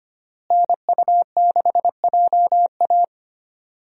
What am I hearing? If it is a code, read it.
Morse code, heard as a keyed tone, about 25 wpm, 700 Hz. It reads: NU6JA
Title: USA Call Signs - 5 Characters - Repeated 2x - 25wpm